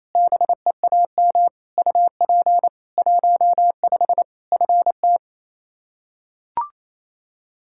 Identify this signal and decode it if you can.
Morse code, heard as a keyed tone, about 28 wpm, 700 Hz. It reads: BEAM UP 15 FT
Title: Practice QSO - Exchange 2 - Info - 28wpm